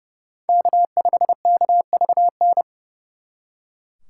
Morse code, heard as a keyed tone, about 36 wpm, 700 Hz. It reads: K5KVN
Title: Speed Racing - USA Call Signs 30wpm